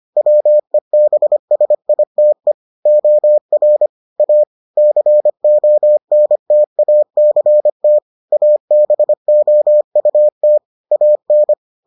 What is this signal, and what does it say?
Morse code, heard as a keyed tone, about 25 wpm, 600 Hz. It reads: WEBSITE OR A CONTACT ABOUT AN